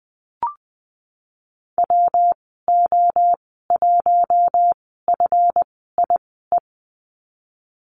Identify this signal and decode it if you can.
Morse code, heard as a keyed tone, about 20 wpm, spaced as for 15 wpm, 700 Hz. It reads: WO1FIE
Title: 179 Lesson 29g - Slash - Call Signs 20wpm